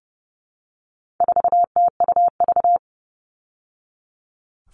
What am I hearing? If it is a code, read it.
Morse code, heard as a keyed tone, about 30 wpm, 700 Hz. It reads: 4TUV
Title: All Cut Numbers - 4-Digit 30wpm